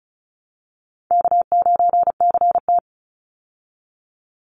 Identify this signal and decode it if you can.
Morse code, heard as a keyed tone, about 34 wpm, 700 Hz. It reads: K9CT
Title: Easy Call Signs Repeated 2x - 35wpm